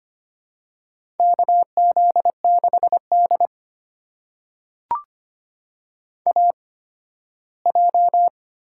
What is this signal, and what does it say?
Morse code, heard as a keyed tone, about 25 wpm, spaced as for 15 wpm, 700 Hz. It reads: KZ6D AJ
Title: USA Call Signs - 4 Characters - 8x Character Spacing - 25wpm